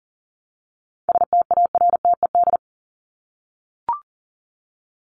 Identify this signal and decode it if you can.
Morse code, heard as a keyed tone, about 40 wpm, 700 Hz. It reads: STARTED
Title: ED Suffix - Top 100 Words with Base Words 40wpm